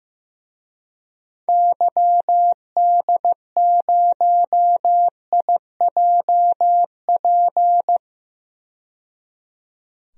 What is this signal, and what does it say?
Morse code, heard as a keyed tone, about 15 wpm, 700 Hz. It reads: YD0IJP
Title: DXCC-100 Call Signs - 6 Characters - 15wpm